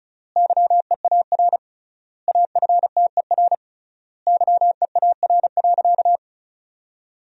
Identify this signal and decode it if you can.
Morse code, heard as a keyed tone, about 36 wpm, 700 Hz. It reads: YEAR AFTER YEAR.
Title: 3x Word-Spacing - Sentences from Top 300 Words - 35wpm